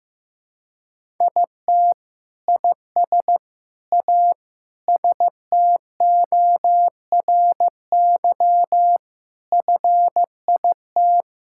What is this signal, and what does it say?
Morse code, heard as a keyed tone, about 15 wpm, 700 Hz. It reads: IT IS A STORY FIT
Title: Sentences from Top 1000 Words - Review Part 5 - 15wpm